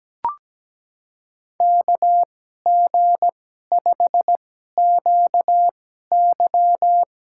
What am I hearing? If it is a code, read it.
Morse code, heard as a keyed tone, about 17 wpm, spaced as for 12 wpm, 700 Hz. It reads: KG5QY
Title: USA Call Signs - 5 Characters - 2x Character Spacing - 17wpm